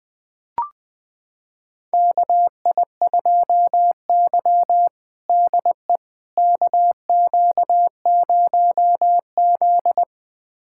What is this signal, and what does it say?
Morse code, heard as a keyed tone, about 20 wpm, 700 Hz. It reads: KI2Y DE KQ0Z
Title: USA Callsign DE Callsign - 4 Characters - 20wpm